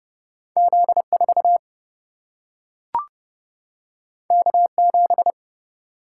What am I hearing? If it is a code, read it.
Morse code, heard as a keyed tone, about 30 wpm, 700 Hz. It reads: Z4 K7
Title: Single-Letter Single-Number 30wpm